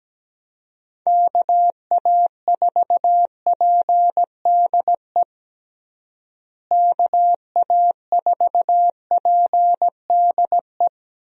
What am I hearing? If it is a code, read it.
Morse code, heard as a keyed tone, about 17 wpm, 700 Hz. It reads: KA4PDE KA4PDE
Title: USA Call Signs - 6 Characters - Repeated 2x - 17wpm